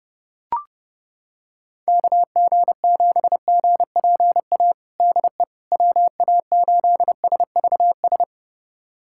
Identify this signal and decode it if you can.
Morse code, heard as a keyed tone, about 30 wpm, 700 Hz. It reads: KG7GPA DE WA8SVS
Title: USA Callsign DE Callsign 30wpm